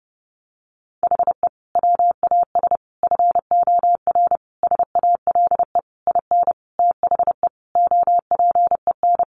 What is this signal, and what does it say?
Morse code, heard as a keyed tone, about 30 wpm, 700 Hz. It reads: HE WAS FOR SALE IN THE OPEN
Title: Sentences from Top 1400 Words 30wpm